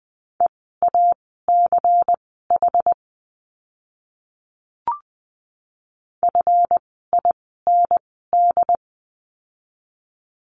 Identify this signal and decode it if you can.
Morse code, heard as a keyed tone, about 20 wpm, spaced as for 14 wpm, 700 Hz. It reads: EACH FIND
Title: 192 Lesson 31f - Letter v - Words 20wpm